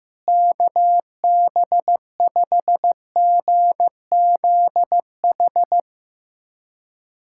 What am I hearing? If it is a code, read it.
Morse code, heard as a keyed tone, about 16 wpm, 700 Hz. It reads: KB5GZH
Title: USA Callsign DE Callsign - 6 Characters - 15wpm